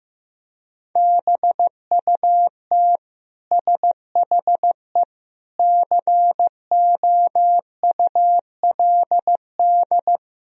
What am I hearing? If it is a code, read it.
Morse code, heard as a keyed tone, about 16 wpm, 700 Hz. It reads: BUT SHE COULD
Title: Sentences from Top 1000 Words - Review Part 4 - 15wpm